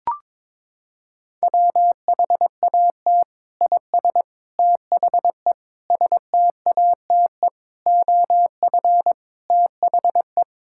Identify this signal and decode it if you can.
Morse code, heard as a keyed tone, about 22 wpm, 700 Hz. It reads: WHAT IS THE STATE OF THE
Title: Sentences from Top 1000 Words - Review Part 5 - 22wpm